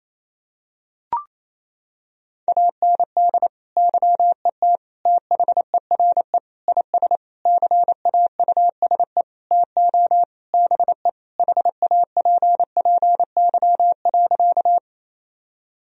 Morse code, heard as a keyed tone, about 28 wpm, 700 Hz. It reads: AND YET THERE IS CAUSE TO BE HAPPY.
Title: Sentences from Top 1000 Words - Review Part 1 - 28wpm